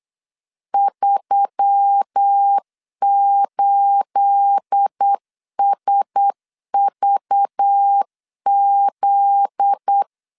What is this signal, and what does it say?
Morse code, heard as a keyed tone, about 8 wpm, 800 Hz. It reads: 38SVZ
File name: CW / DET / 1 9 1500C S B 8wpm 5624